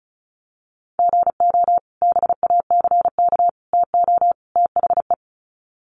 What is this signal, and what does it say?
Morse code, heard as a keyed tone, about 34 wpm, 700 Hz. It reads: GO BACK TO THE
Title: Sets of 4 Words 35wpm